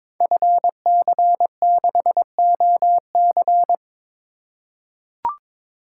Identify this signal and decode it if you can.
Morse code, heard as a keyed tone, about 22 wpm, 700 Hz. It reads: FC6OC
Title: Code Groups - 5 Letter:Numbers - 22wpm